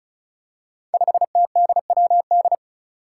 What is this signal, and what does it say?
Morse code, heard as a keyed tone, about 36 wpm, 700 Hz. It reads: 5TDWD
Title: Five Challenging Letter-Number Combinations 35wpm